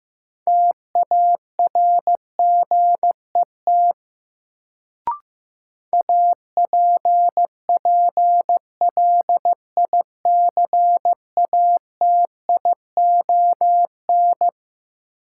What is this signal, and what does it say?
Morse code, heard as a keyed tone, about 15 wpm, 700 Hz. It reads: TARGET APPLICATION
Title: Sentences from Top 1200 Words - New Word Drill Alt - 15wpm